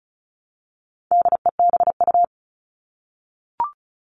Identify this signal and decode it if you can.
Morse code, heard as a keyed tone, about 34 wpm, 700 Hz. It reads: DEBU
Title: All Cut Numbers - 4-Digit 35wpm